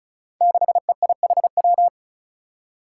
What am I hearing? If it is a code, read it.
Morse code, heard as a keyed tone, about 36 wpm, 700 Hz. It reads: 6EIHW
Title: Five Challenging Letter-Number Combinations 35wpm